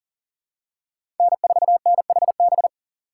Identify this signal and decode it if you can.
Morse code, heard as a keyed tone, about 40 wpm, 700 Hz. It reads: N4NHB
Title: USA Call Signs - 5 Characters - 3x Character Spacing - 40wpm